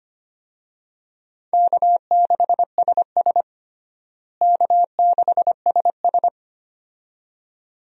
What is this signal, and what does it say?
Morse code, heard as a keyed tone, about 25 wpm, 700 Hz. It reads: K6SS K6SS
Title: USA Call Signs - 4 Characters - Repeated 2x - 25wpm